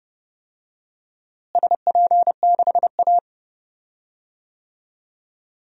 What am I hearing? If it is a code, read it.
Morse code, heard as a keyed tone, about 30 wpm, 700 Hz. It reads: SP6A
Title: DXCC-100 Call Signs - 4 Characters - Repeated 2x - 30wpm